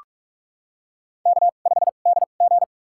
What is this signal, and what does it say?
Morse code, heard as a keyed tone, about 46 wpm, spaced as for 33 wpm, 700 Hz. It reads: K5DG
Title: USA Call Signs - 4 Characters - 2x Character Spacing - 45wpm